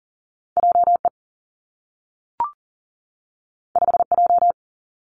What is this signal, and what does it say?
Morse code, heard as a keyed tone, about 41 wpm, 700 Hz. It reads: JE 5J
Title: Two Challenging Letter-Number Combinations 40wpm